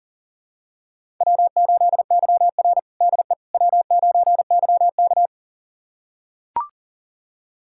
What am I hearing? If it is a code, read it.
Morse code, heard as a keyed tone, about 40 wpm, 700 Hz. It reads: W8YR DE W9YK
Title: USA Callsign DE Callsign - 4 Characters - 40wpm